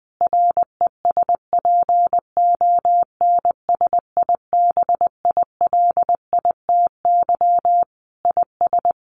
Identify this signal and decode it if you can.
Morse code, heard as a keyed tone, about 20 wpm, 700 Hz. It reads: RESPONSIBILITY IS
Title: Sentences from Top 1200 Words 20wpm